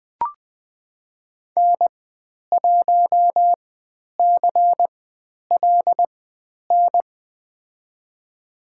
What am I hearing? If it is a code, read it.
Morse code, heard as a keyed tone, about 20 wpm, spaced as for 10 wpm, 700 Hz. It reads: N1CLN
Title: 095 Lesson 16g - Letter c - Call Signs 20wpm